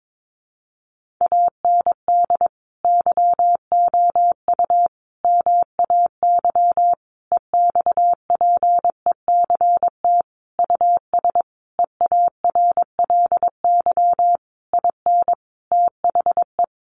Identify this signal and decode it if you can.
Morse code, heard as a keyed tone, about 22 wpm, 700 Hz. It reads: AND YOU MAY EXPECT US EARLY IN THE
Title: Sentences from Top 1500 Words 22wpm